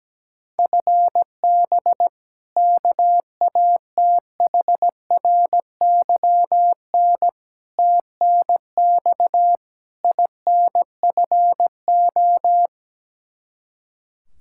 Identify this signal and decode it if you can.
Morse code, heard as a keyed tone, about 17 wpm, 700 Hz. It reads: FB KATHRYN TNX INFO
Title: Practice QSO - Exchange 3 - More Info - 17wpm